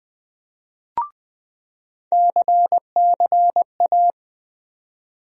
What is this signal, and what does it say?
Morse code, heard as a keyed tone, about 20 wpm, 700 Hz. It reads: CCA
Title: USA Call Sign Postfixes - 3 Letters 20wpm